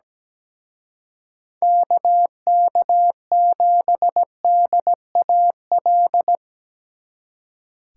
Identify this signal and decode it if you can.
Morse code, heard as a keyed tone, about 19 wpm, 700 Hz. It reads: KK7DAL
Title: Speed Racing - USA 2x3 Call Signs 17wpm